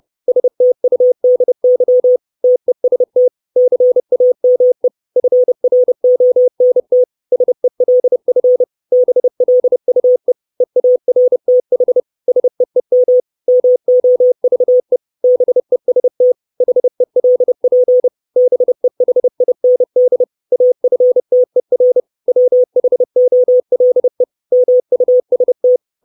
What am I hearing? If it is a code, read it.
Morse code, heard as a keyed tone, about 30 wpm, 500 Hz. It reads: STUDY TEST CAME FRONT SELF BLUE EARTH SEEM MOVE BEST HELP BEHIND AFTER WHOLE MUST
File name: 632 Words 30WPM